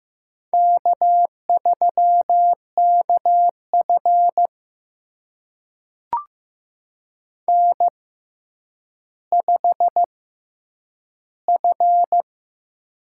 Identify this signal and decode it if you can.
Morse code, heard as a keyed tone, about 15 wpm, spaced as for 9 wpm, 700 Hz. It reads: K3KF N5F
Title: USA Call Signs - 6x Character Spacing - 15wpm